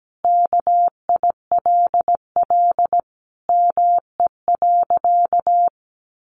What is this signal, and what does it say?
Morse code, heard as a keyed tone, about 17 wpm, 700 Hz. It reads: KILL ME.
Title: Sentences from Top 1400 Words 17wpm